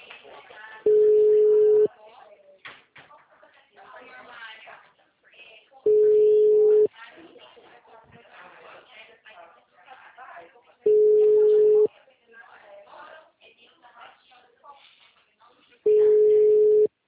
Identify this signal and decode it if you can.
Ringback tone (Europe, 425 Hz), with speech.